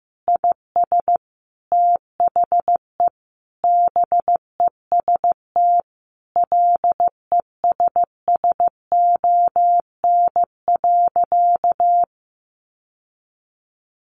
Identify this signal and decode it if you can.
Morse code, heard as a keyed tone, about 15 wpm, 700 Hz. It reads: IS THE BEST LESSON.
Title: Sentences from Top 1400 Words 15wpm